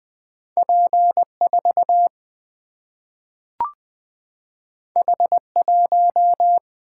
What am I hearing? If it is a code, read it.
Morse code, heard as a keyed tone, about 20 wpm, 700 Hz. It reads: P 4 H 1
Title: Single-Letter Single-Number 20wpm